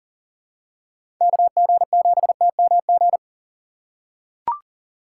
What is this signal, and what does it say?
Morse code, heard as a keyed tone, about 40 wpm, 700 Hz. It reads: KG7TMG
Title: USA Call Signs - 6 Characters - 4x Character Spacing - 40wpm